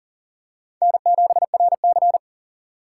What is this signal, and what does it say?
Morse code, heard as a keyed tone, about 40 wpm, 700 Hz. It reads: N7RC
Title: USA Call Signs - 4 Characters - 2x Character Spacing - 40wpm